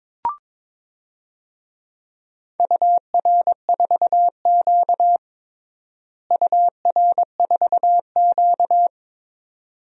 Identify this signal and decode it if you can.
Morse code, heard as a keyed tone, about 22 wpm, 700 Hz. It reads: UR4Q UR4Q
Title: DXCC-100 Call Signs - 4 Characters - Repeated 2x - 22wpm